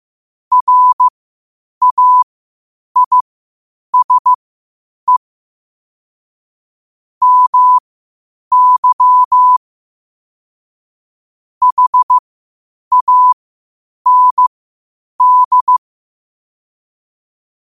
Morse code, heard as a keyed tone, about 15 wpm, spaced as for 8 wpm, 1000 Hz. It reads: RAISE MY HAND
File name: qotdfiles / 8wpm